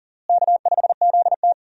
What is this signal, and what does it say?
Morse code, heard as a keyed tone, about 37 wpm, 700 Hz. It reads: K5ZT
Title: USA Call Signs - 4 Characters - Repeated 2x - 40wpm